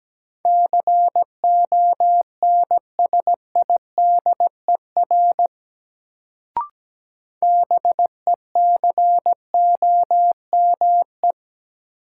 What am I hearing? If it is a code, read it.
Morse code, heard as a keyed tone, about 17 wpm, 700 Hz. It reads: CONSIDER BECOME
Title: Sentences from Top 600 Words - New Word Drill - 17wpm